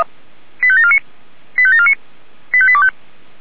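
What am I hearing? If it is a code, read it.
Five-tone selective call (ZVEI), heard as quick runs of five tones, tones of about 70 ms. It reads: 85749 85749 85825
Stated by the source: ZVEI five-tone selective call received by SDR.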